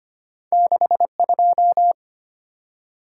Morse code, heard as a keyed tone, about 25 wpm, 700 Hz. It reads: 62
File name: Two Letter-Number Combinations - 25wpm